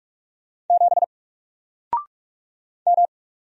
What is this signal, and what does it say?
Morse code, heard as a keyed tone, about 45 wpm, 700 Hz. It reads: 7 M
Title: Single Letter-Number Alt 45wpm